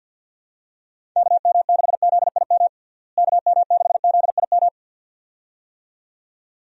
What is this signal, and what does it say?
Morse code, heard as a keyed tone, about 51 wpm, 700 Hz. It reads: KM6ZIM KM6ZIM
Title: USA Call Signs - 6 Characters - Repeated 2x - 50wpm